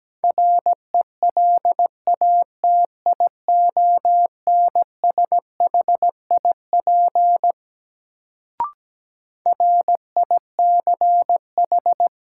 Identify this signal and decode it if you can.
Morse code, heard as a keyed tone, about 17 wpm, 700 Hz. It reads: RELATIONSHIP RICH
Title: Sentences from Top 1100 Words - New Word Drill Alt - 17wpm